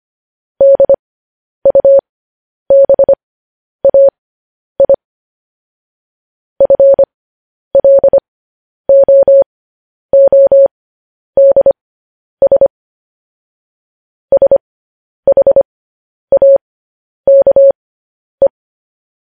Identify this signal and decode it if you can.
Morse code, heard as a keyed tone, about 25 wpm, spaced as for 10 wpm, 550 Hz. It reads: DUBAI FLOODS SHAKE